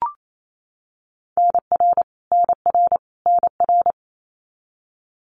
Morse code, heard as a keyed tone, about 28 wpm, 700 Hz. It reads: NR NR NR
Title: Abbreviations Repeated 3x - 28wpm